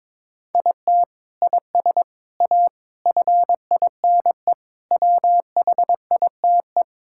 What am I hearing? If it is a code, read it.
Morse code, heard as a keyed tone, about 21 wpm, 700 Hz. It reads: IT IS A FINE WHITE
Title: Sentences from Top 1600 Words 22wpm